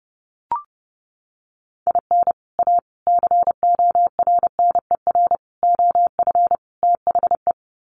Morse code, heard as a keyed tone, about 30 wpm, 700 Hz. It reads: IN A CORNER OF THE
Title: Sets of 5 Words Encore2 30wpm